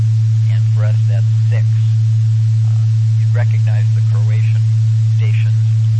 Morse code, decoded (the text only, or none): none